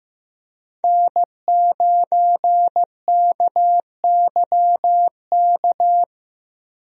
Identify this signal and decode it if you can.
Morse code, heard as a keyed tone, about 15 wpm, 700 Hz. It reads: N9KYK
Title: USA Call Signs - 5 Characters - Repeated 2x - 15wpm